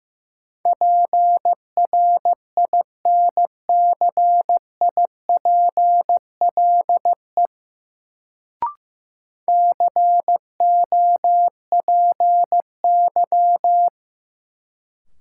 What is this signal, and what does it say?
Morse code, heard as a keyed tone, about 16 wpm, 700 Hz. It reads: PRINCIPLE COPY
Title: Sentences from Top 1400 Words - New Word Drill - 15wpm